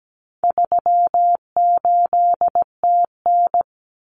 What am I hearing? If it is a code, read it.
Morse code, heard as a keyed tone, about 17 wpm, 700 Hz. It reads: 38TN
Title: Common Cut Numbers - 4-Digit 17wpm